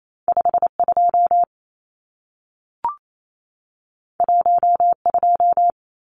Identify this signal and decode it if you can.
Morse code, heard as a keyed tone, about 28 wpm, 700 Hz. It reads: 52 12
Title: Numbers - 2-Digit 28wpm